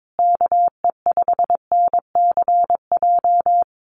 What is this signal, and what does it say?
Morse code, heard as a keyed tone, about 22 wpm, 700 Hz. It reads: KE5NCJ
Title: USA Call Signs Repeated 3x 22wpm